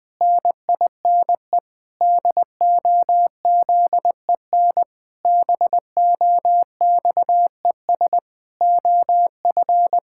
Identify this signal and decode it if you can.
Morse code, heard as a keyed tone, about 21 wpm, 700 Hz. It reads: NINE DOZEN BOXES OF
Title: Pangram Challenge v2 - 20wpm